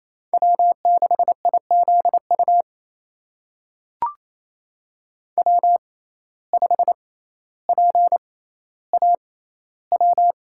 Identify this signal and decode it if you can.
Morse code, heard as a keyed tone, about 28 wpm, spaced as for 14 wpm, 700 Hz. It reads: W6IZU W5PAW
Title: USA Call Signs - 5 Characters - 6x Character Spacing - 28wpm